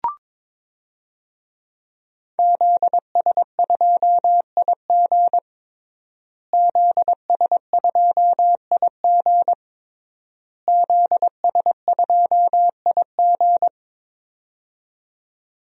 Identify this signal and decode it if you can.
Morse code, heard as a keyed tone, about 21 wpm, 700 Hz. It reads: ZS2IG ZS2IG ZS2IG
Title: DXCC-100 Call Signs - 5 Characters - Repeated 3x - 22wpm